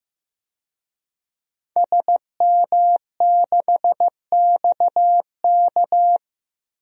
Morse code, heard as a keyed tone, about 15 wpm, 700 Hz. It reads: SM6XK
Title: DXCC-100 Call Signs - 5 Characters - 15wpm